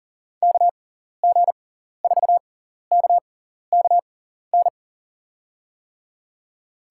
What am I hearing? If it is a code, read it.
Morse code, heard as a keyed tone, about 40 wpm, spaced as for 14 wpm, 700 Hz. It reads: KG4KKN
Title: USA Call Signs - 6 Characters - 6x Character Spacing - 40wpm